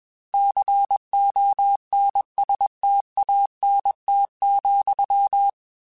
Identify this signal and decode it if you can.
Morse code, heard as a keyed tone, about 21 wpm, 800 Hz. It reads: CONSTANT,
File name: Technical Subjects / Technical Subjects-mp3s / XC @20